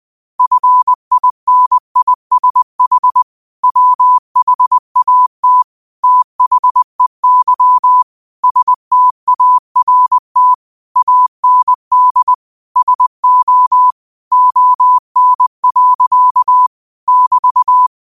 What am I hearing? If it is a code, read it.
Morse code, heard as a keyed tone, about 20 wpm, 1000 Hz. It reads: FINISH WHAT THEY START AND SO ON. <BT>
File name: qotdfiles / 20wpm